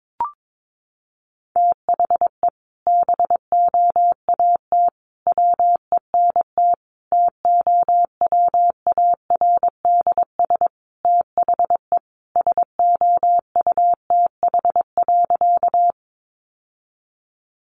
Morse code, heard as a keyed tone, about 22 wpm, 700 Hz. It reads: THE BOAT WENT TOWARDS THE SOUTH.
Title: Sentences from Top 1100 Words 22wpm